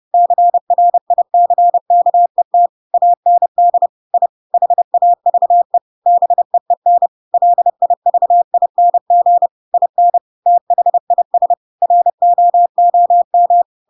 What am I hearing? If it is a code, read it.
Morse code, heard as a keyed tone, about 30 wpm, 700 Hz. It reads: CRICKET AND I HAVE BEEN LIVING IN THIS ROOM
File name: Pinocchio / Pinocchio ch4 30 30 ik1hge